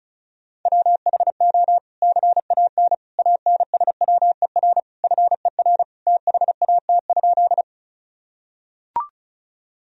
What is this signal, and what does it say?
Morse code, heard as a keyed tone, about 34 wpm, 700 Hz. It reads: WHO CAN ANSWER FER THAT?
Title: Sentences from Top 300 Words and Abbreviations 35wpm